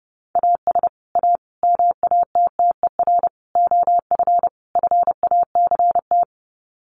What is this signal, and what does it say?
Morse code, heard as a keyed tone, about 28 wpm, 700 Hz. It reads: AS A MATTER OF FACT
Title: Sets of 5 Words 30wpm